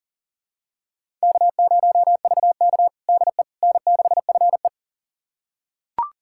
Morse code, heard as a keyed tone, about 40 wpm, 700 Hz. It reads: K0VK DE N6FE
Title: USA Callsign DE Callsign - 4 Characters - 40wpm